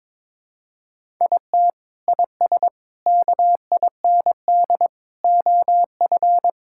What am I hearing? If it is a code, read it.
Morse code, heard as a keyed tone, about 22 wpm, 700 Hz. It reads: IT IS KIND OF
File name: practice / 22wpm / Sentences with 2-Word Phrases - 22wpm